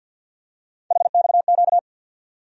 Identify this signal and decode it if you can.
Morse code, heard as a keyed tone, about 52 wpm, 700 Hz. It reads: HXQ
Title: USA Call Sign Postfixes - 3 Letters 50wpm